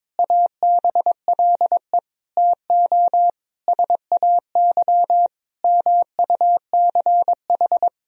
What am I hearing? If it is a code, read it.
Morse code, heard as a keyed tone, about 21 wpm, 700 Hz. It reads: ABLE TO SAY MUCH
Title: Sentences from Top 1000 Words - Review Part 2 - 22wpm